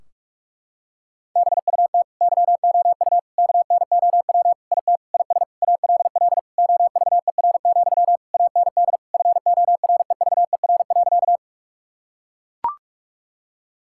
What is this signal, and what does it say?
Morse code, heard as a keyed tone, about 41 wpm, 700 Hz. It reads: BUT YOU KNOW IT IS ALL OVER, AND FOREVER.
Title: Sentences from Top 1600 Words 45wpm